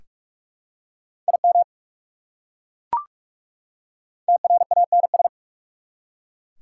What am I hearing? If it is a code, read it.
Morse code, heard as a keyed tone, about 44 wpm, 690 Hz. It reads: IM TRANS
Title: Prefixes and Suffixes 45wpm